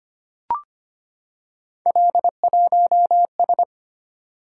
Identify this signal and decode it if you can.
Morse code, heard as a keyed tone, about 25 wpm, 700 Hz. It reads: L1S
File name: Three Letter-Number Combinations - 25wpm